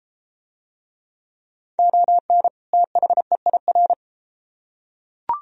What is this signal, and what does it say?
Morse code, heard as a keyed tone, about 31 wpm, 700 Hz. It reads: ON THEIR